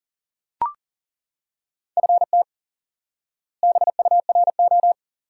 Morse code, heard as a keyed tone, about 40 wpm, 700 Hz. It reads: FT BURO
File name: IQE Course 3 - Lesson 89 - YAGI - Two Elements - 40wpm